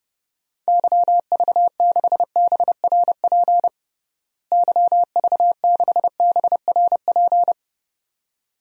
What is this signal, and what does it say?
Morse code, heard as a keyed tone, about 30 wpm, 700 Hz. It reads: YV6BRP YV6BRP
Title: DXCC-100 Call Signs - 6 Characters - Repeated 2x - 30wpm